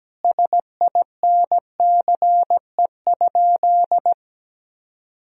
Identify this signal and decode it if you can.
Morse code, heard as a keyed tone, about 17 wpm, 700 Hz. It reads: SINCE?
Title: Sentences from Top 1000 Words - Review Part 4 - 17wpm